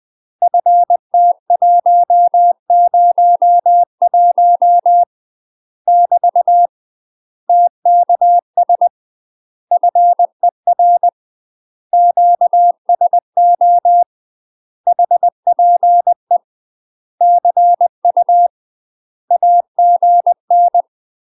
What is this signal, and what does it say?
Morse code, heard as a keyed tone, about 20 wpm, 700 Hz. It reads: FT101 <BT> TKS FER QSO HPE CU AGN
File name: UK MASTER QSOs TRAINING/20WPM